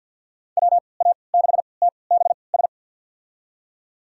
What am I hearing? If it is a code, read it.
Morse code, heard as a keyed tone, about 47 wpm, spaced as for 28 wpm, 700 Hz. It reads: WA6TBS
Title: USA Call Signs - 6 Characters - 3x Character Spacing - 50wpm